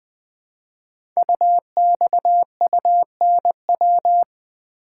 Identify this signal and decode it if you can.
Morse code, heard as a keyed tone, about 20 wpm, 700 Hz. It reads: UXUNW
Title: Code Groups - 5 Letters - 20wpm